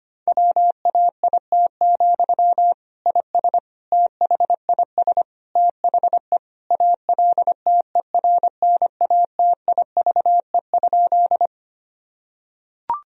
Morse code, heard as a keyed tone, about 25 wpm, 700 Hz. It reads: WAIT, IS THIS THE ALTERNATIVE?
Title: Sentences from Top 1000 Words - Review Part 4 - 25wpm